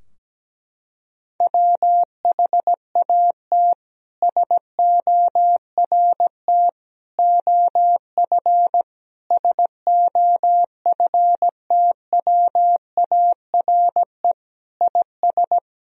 Morse code, heard as a keyed tone, about 17 wpm, 700 Hz. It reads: WHAT SORT OF SOFTWARE IS
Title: Sentences from Top 1000 Words - Review Part 3 - 17wpm